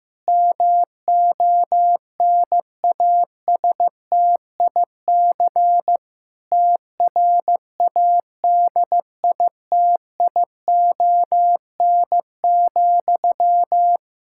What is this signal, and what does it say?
Morse code, heard as a keyed tone, about 15 wpm, 700 Hz. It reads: MONASTIC TRADITION,
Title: Pangram Challenge v2 - 15wpm